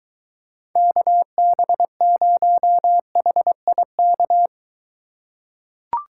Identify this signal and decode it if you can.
Morse code, heard as a keyed tone, about 23 wpm, 700 Hz. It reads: KB0HIK